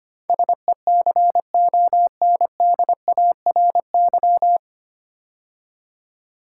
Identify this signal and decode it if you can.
Morse code, heard as a keyed tone, about 25 wpm, 690 Hz. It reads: SECONDARY
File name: practice / 25wpm / 9 Letter Words - First 3 Letters Given - 25wpm